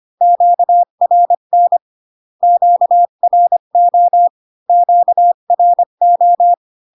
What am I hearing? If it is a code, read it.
Morse code, heard as a keyed tone, about 25 wpm, 700 Hz. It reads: QRN QRO QRO